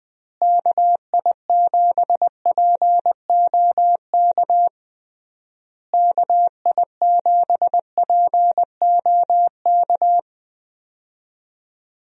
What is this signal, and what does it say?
Morse code, heard as a keyed tone, about 20 wpm, 700 Hz. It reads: KI7POK KI7POK
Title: USA Call Signs - 6 Characters - Repeated 3x - 20wpm